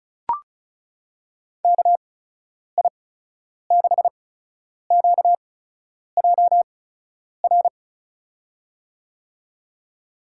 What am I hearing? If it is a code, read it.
Morse code, heard as a keyed tone, about 36 wpm, spaced as for 10 wpm, 700 Hz. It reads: KI6QJR
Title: USA Call Signs - 8x Character Spacing - 35wpm